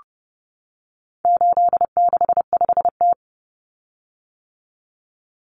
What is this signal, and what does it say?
Morse code, heard as a keyed tone, about 30 wpm, 700 Hz. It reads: 865T
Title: Common Cut Numbers - 4-Digit 30wpm